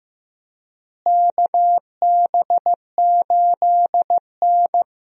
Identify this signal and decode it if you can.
Morse code, heard as a keyed tone, about 14 wpm, 700 Hz. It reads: KB8N
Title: USA Call Signs - 4 Characters - Repeated 3x - 15wpm